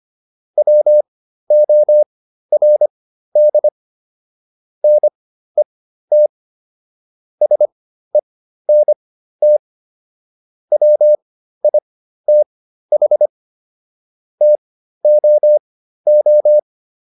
Morse code, heard as a keyed tone, about 25 wpm, spaced as for 13 wpm, 600 Hz. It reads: WORD NET SENT WITH TOO